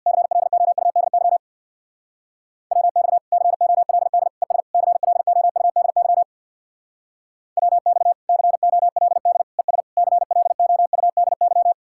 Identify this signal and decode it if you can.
Morse code, heard as a keyed tone, about 63 wpm, 700 Hz. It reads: CLOUDY WX COLD ES CLOUDY WX COLD ES CLOUDY